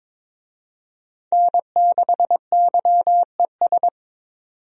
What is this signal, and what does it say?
Morse code, heard as a keyed tone, about 22 wpm, 700 Hz. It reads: N6YES
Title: USA Call Signs - 5 Characters - 22wpm